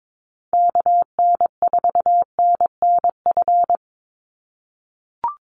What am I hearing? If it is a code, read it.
Morse code, heard as a keyed tone, about 22 wpm, 700 Hz. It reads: KN4NNF
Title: USA Call Signs 22wpm